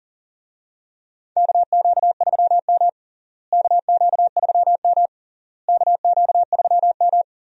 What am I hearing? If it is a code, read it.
Morse code, heard as a keyed tone, about 40 wpm, 700 Hz. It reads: KQ3M KQ3M KQ3M
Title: USA Call Signs - 4 Characters - Repeated 3x - 40wpm